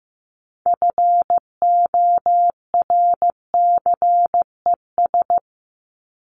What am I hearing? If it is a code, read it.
Morse code, heard as a keyed tone, about 15 wpm, 700 Hz. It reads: FORCES
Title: ES:S Suffix - Top 100 Words with Base Words 15wpm